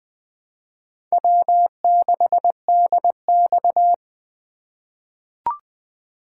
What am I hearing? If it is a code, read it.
Morse code, heard as a keyed tone, about 21 wpm, 700 Hz. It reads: W6DX
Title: USA Call Signs - 4 Characters - 4x Character Spacing - 20wpm